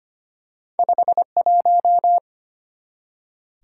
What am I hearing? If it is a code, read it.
Morse code, heard as a keyed tone, about 25 wpm, 700 Hz. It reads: 51
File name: Two Letter-Number Combinations - 25wpm